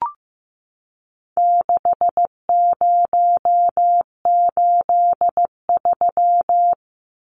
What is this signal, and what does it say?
Morse code, heard as a keyed tone, about 15 wpm, 700 Hz. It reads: 6083
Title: Numbers - 4-Digit 15wpm